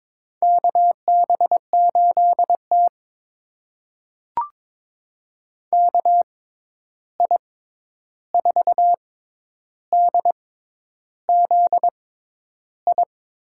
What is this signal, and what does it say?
Morse code, heard as a keyed tone, about 22 wpm, spaced as for 10 wpm, 700 Hz. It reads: KB8T KI4DZI
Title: USA Call Signs - 6x Character Spacing - 22wpm